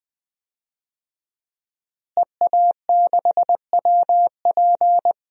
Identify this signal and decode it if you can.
Morse code, heard as a keyed tone, about 21 wpm, 700 Hz. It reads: EA6WP
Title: DXCC-100 Call Signs - 5 Characters - Repeated 3x - 20wpm